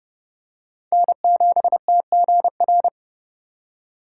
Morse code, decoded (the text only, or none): N7TGR